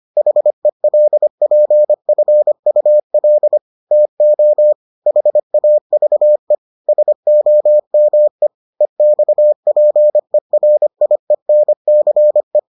Morse code, heard as a keyed tone, about 25 wpm, 600 Hz. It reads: HELPFUL TO HAVE SOME EXPERIENCE